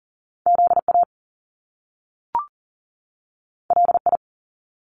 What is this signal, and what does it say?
Morse code, heard as a keyed tone, about 40 wpm, 700 Hz. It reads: ZA LI
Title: Top Two Letter Combinations in Words 40wpm